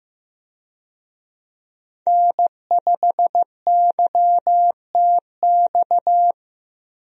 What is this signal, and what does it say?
Morse code, heard as a keyed tone, about 15 wpm, 700 Hz. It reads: N5YTX
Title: USA Call Signs - 5 Characters - Repeated 3x - 15wpm